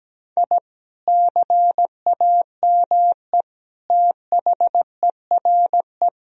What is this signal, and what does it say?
Morse code, heard as a keyed tone, about 17 wpm, 700 Hz. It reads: I CAME THERE
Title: Sentences from Top 1000 Words - Review Part 4 - 17wpm